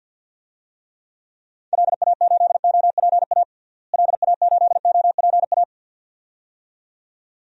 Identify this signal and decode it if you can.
Morse code, heard as a keyed tone, about 47 wpm, 700 Hz. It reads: LA8OPA LA8OPA
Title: DXCC-100 Call Signs - 6 Characters - Repeated 2x - 50wpm